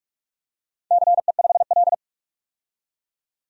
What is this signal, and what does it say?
Morse code, heard as a keyed tone, about 46 wpm, 700 Hz. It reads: CE5L
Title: DXCC-100 Call Signs - 4 Characters - 45wpm